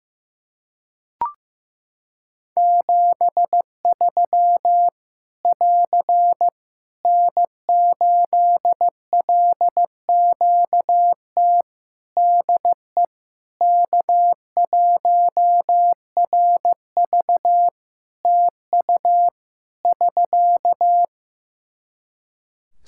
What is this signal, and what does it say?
Morse code, heard as a keyed tone, about 15 wpm, 700 Hz. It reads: 73 <AR> N8LQT DE K1RV TU <SK>
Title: K1USN SST QSO Practice 15wpm